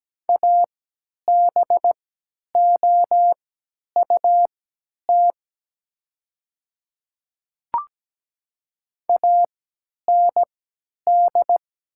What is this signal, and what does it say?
Morse code, heard as a keyed tone, about 17 wpm, spaced as for 10 wpm, 700 Hz. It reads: ABOUT AND
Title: 3x Character Spacing - Top 200 Words - 17wpm